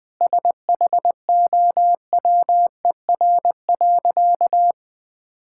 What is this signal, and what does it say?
Morse code, heard as a keyed tone, about 20 wpm, 700 Hz. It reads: SHOWER.
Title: Sentences from Top 1600 Words 20wpm